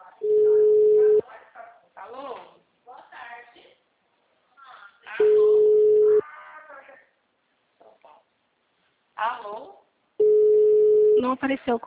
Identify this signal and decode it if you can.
Ringback tone (Europe, 425 Hz), with speech.